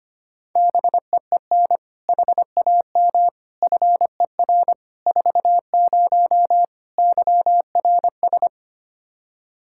Morse code, heard as a keyed tone, about 25 wpm, 700 Hz. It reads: BEEN HAM FER 40 YRS
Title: Practice QSO - Exchange 3 - More Info - 25wpm